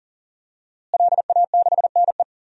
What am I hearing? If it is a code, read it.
Morse code, heard as a keyed tone, about 40 wpm, 700 Hz. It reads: LA6NE
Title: DXCC-100 Call Signs - 5 Characters - Repeated 3x - 40wpm